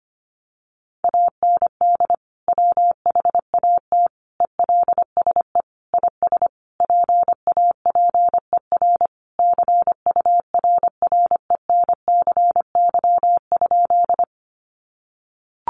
Morse code, heard as a keyed tone, about 25 wpm, 700 Hz. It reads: AND WHAT ELSE IS PAPER CURRENCY?
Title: Sentences from Top 1500 Words 25wpm